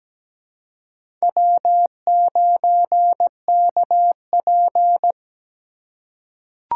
Morse code, heard as a keyed tone, about 17 wpm, 700 Hz. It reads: W9KP
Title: USA Call Signs - 4 Characters - 4x Character Spacing - 17wpm